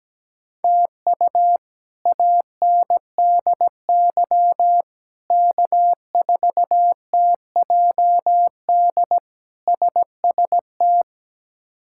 Morse code, heard as a keyed tone, about 17 wpm, 700 Hz. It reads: TU ANDY K4TJD SST
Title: K1USN SST QSO Practice - 17wpm